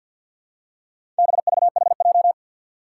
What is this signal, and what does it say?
Morse code, heard as a keyed tone, about 48 wpm, 700 Hz. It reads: BVHJ
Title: Four Challenging Letter-Number Combinations 50wpm